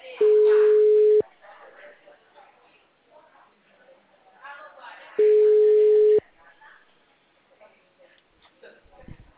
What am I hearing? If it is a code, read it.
Ringback tone (Europe, 425 Hz), with speech.
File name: out- -2013- - - .1962